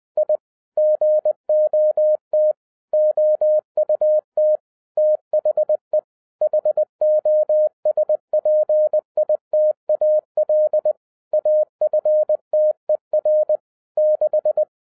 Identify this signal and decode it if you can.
Morse code, heard as a keyed tone, about 20 wpm, 610 Hz. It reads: I GOT OUT THE HOSPITAL AFTER 6
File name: 2011-07-mp3 / n58 BACK FROM A BIG ACCIDENT 20wpm